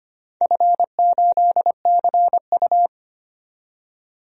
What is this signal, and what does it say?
Morse code, heard as a keyed tone, about 25 wpm, 700 Hz. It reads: F8CU
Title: DXCC-100 Call Signs - 4 Characters - 25wpm